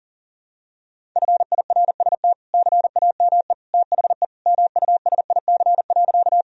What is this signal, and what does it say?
Morse code, heard as a keyed tone, about 37 wpm, 700 Hz. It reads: FIRST CAME THE MUSIC.
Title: Sentences from Top 1000 Words - Review Part 1 - 40wpm